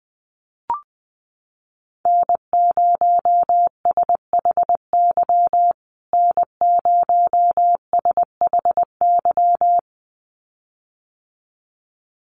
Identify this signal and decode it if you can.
Morse code, heard as a keyed tone, about 20 wpm, 700 Hz. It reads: N0SHY N0SHY
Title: USA Call Signs Repeated 2x 20wpm